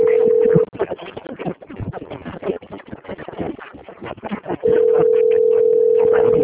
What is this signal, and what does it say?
Ringback tone (North America), 440 + 480 Hz.
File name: out- - - - - .17867